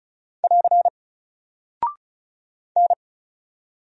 Morse code, heard as a keyed tone, about 36 wpm, 700 Hz. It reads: <AR> N
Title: Single Character (Letter:Number:Punctuation:Prosign) Alt 35wpm